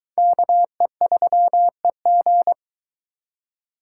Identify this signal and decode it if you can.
Morse code, heard as a keyed tone, about 22 wpm, 700 Hz. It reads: KE3EG